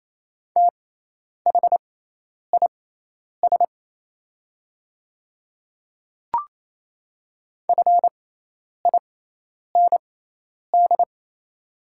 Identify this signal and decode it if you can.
Morse code, heard as a keyed tone, about 28 wpm, spaced as for 10 wpm, 690 Hz. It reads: THIS FIND
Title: 6x Character Spacing - Top 100 Words - 28wpm